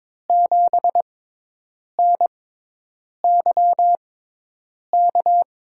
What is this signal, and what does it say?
Morse code, heard as a keyed tone, about 23 wpm, spaced as for 8 wpm, 700 Hz. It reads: 7NYK
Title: USA Call Signs - 5 Characters - 6x Character Spacing - 22wpm